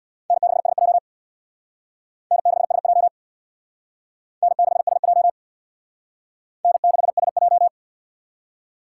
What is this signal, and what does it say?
Morse code, heard as a keyed tone, about 63 wpm, 700 Hz. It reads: N6SJ N6SJ N6SJ N6SJ